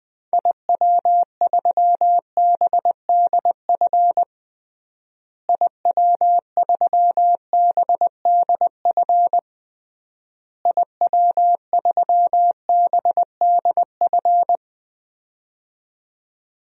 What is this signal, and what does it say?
Morse code, heard as a keyed tone, about 20 wpm, 700 Hz. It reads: IW3BDF IW3BDF IW3BDF
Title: DXCC-100 Call Signs - 6 Characters - Repeated 3x - 20wpm